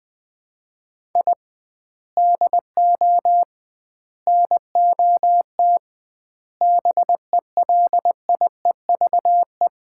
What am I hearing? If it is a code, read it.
Morse code, heard as a keyed tone, about 20 wpm, 700 Hz. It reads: I DO NOT BELIEVE
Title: 2x Word-Spacing - Sets of 4 Words - 20wpm